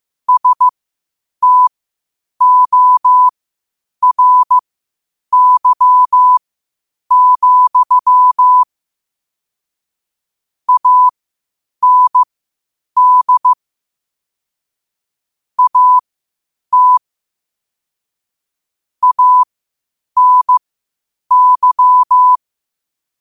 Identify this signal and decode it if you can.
Morse code, heard as a keyed tone, about 15 wpm, spaced as for 8 wpm, 1000 Hz. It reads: STORY, AND AT ANY
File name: qotdfiles / 8wpm